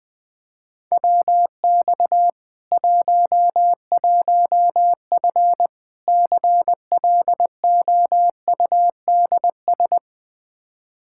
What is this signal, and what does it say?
Morse code, heard as a keyed tone, about 20 wpm, 700 Hz. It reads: WX 11F CLOUDS
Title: Practice QSO - Exchange 2 - Info - 20wpm